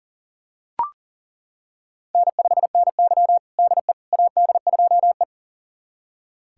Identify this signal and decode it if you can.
Morse code, heard as a keyed tone, about 40 wpm, 700 Hz. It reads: N5NY DE AD2E
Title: USA Callsign DE Callsign - 4 Characters - 40wpm